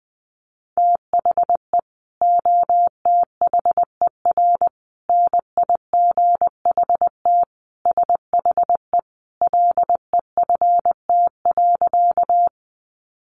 Morse code, heard as a keyed tone, about 20 wpm, 700 Hz. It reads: THE OTHER NIGHT SHE LEFT.
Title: Sentences from Top 300 Words 20wpm